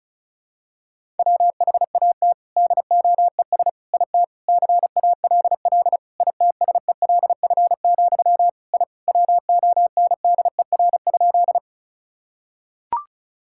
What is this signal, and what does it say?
Morse code, heard as a keyed tone, about 36 wpm, 700 Hz. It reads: WHAT DOES IT CALL ITSELF, I WONDER?
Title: Sentences from Top 1000 Words - Review Part 5 - 35wpm